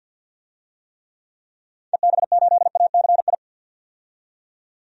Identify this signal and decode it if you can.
Morse code, heard as a keyed tone, about 52 wpm, 700 Hz. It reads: EB8ACI
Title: DXCC-100 Call Signs - 6 Characters - 50wpm